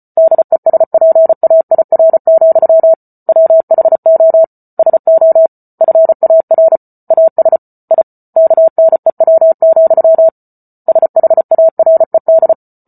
Morse code, heard as a keyed tone, about 34 wpm, 650 Hz. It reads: DESPAIR, WHO SO FAR AS I KNEW, SHARED